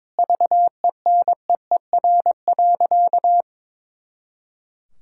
Morse code, heard as a keyed tone, about 22 wpm, 700 Hz. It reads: VENEER.
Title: Pangram Challenge v2 - 22wpm